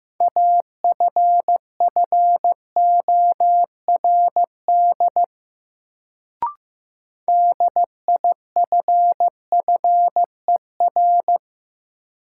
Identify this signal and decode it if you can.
Morse code, heard as a keyed tone, about 15 wpm, 700 Hz. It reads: AFFORD DIFFER
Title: Sentences from Top 1100 Words - New Word Drill Alt - 15wpm